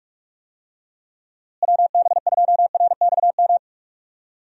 Morse code, heard as a keyed tone, about 46 wpm, 700 Hz. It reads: WB2RXM
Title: USA Call Signs - 6 Characters - Repeated 2x - 45wpm